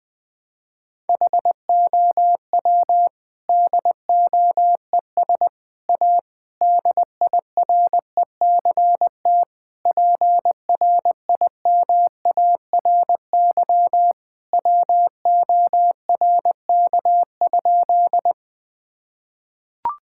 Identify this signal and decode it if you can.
Morse code, heard as a keyed tone, about 20 wpm, 700 Hz. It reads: HOW DOES A DIRECT PRIMARY WORK?
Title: Sentences from Top 1000 Words - Review Part 5 - 20wpm